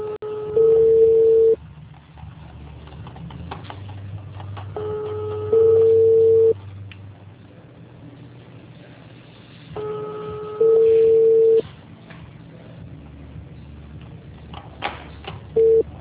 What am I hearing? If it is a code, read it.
Ringback tone (China, 450 Hz).
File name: out- -2013- - - .7092